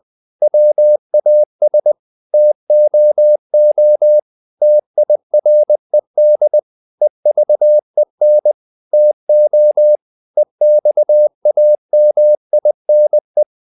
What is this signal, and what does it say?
Morse code, heard as a keyed tone, about 20 wpm, 600 Hz. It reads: WAS TOO TIRED EVEN TO EXAMINE
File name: the old man and the sea / the old man and the sea part 1 20wpm